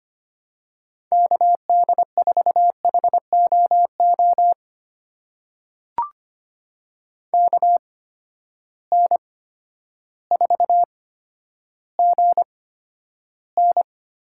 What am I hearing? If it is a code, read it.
Morse code, heard as a keyed tone, about 25 wpm, spaced as for 11 wpm, 700 Hz. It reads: KD4HOO KN4GN
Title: USA Call Signs - 6 Characters - 8x Character Spacing - 25wpm